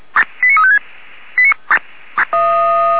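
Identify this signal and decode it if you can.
Five-tone selective call (ZVEI), heard as quick runs of five tones, tones of about 70 ms. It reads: 89356
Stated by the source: ZVEI five-tone selective call received by SDR on the Feuerwehr Orlamünde (fire brigade) audio server.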